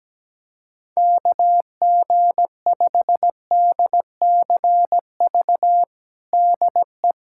Morse code, heard as a keyed tone, about 17 wpm, 700 Hz. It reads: KG5DCV DE
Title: USA Callsign DE Callsign - 6 Characters - 17wpm